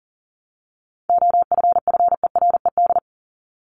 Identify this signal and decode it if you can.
Morse code, heard as a keyed tone, about 40 wpm, 700 Hz. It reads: OFFERED
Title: ED Suffix - Top 100 Words with Base Words 40wpm